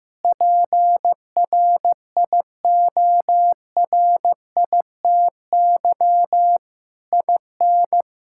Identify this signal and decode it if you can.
Morse code, heard as a keyed tone, about 15 wpm, 700 Hz. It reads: PRIORITY IN
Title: Sentences from Top 1600 Words 15wpm